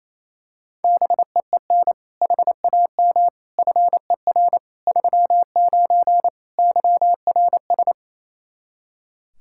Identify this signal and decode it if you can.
Morse code, heard as a keyed tone, about 28 wpm, 700 Hz. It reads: BEEN HAM FER 39 YRS
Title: Practice QSO - Exchange 3 - More Info - 28wpm